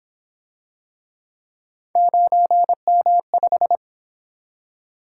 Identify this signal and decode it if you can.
Morse code, heard as a keyed tone, about 28 wpm, 700 Hz. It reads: 9M5